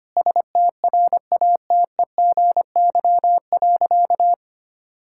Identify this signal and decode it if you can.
Morse code, heard as a keyed tone, about 25 wpm, 700 Hz. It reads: STRATEGY.
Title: Sentences from Top 1000 Words - Review Part 5 - 25wpm